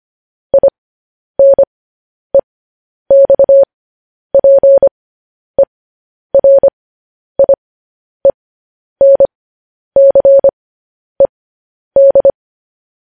Morse code, heard as a keyed tone, about 25 wpm, spaced as for 10 wpm, 550 Hz. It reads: INEXPERIENCED